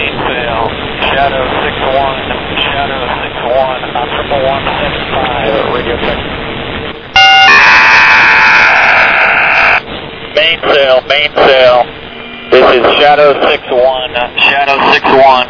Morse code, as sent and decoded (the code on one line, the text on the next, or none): none